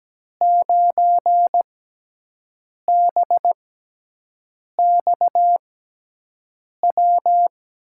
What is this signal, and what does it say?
Morse code, heard as a keyed tone, about 17 wpm, spaced as for 6 wpm, 700 Hz. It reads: 9BXW
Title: USA Call Signs - 6 Characters - 6x Character Spacing - 17wpm